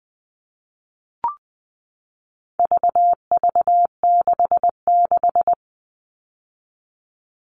Morse code, heard as a keyed tone, about 20 wpm, 700 Hz. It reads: VV66
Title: All Cut Numbers - 4-Digit 20wpm